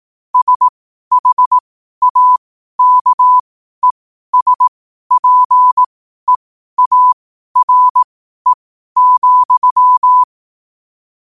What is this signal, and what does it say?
Morse code, heard as a keyed tone, about 18 wpm, spaced as for 12 wpm, 1000 Hz. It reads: SHAKESPEARE,
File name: qotdfiles / 12wpm